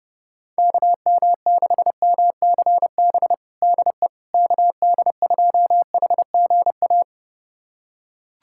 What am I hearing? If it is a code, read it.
Morse code, heard as a keyed tone, about 30 wpm, 700 Hz. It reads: KM6MCB DE KD2HGA
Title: USA Callsign DE Callsign 30wpm